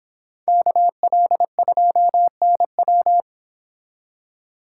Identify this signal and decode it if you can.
Morse code, heard as a keyed tone, about 26 wpm, 700 Hz. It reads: KL2NW